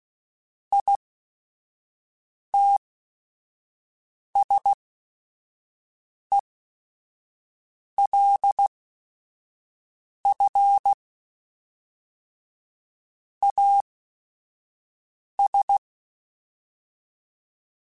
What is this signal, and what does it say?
Morse code, heard as a keyed tone, about 16 wpm, spaced as for 5 wpm, 800 Hz. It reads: ITSELF AS
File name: Technical Subjects / Technical Subjects-mp3s / XC @5